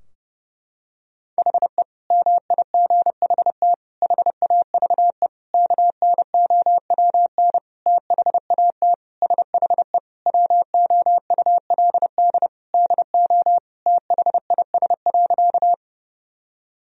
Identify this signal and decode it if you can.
Morse code, heard as a keyed tone, about 30 wpm, 700 Hz. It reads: HE MIGHT HAVE KNOWN THAT SHE WOULD DO THIS.
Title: Sentences from Top 1000 Words - Review Part 3 - 30wpm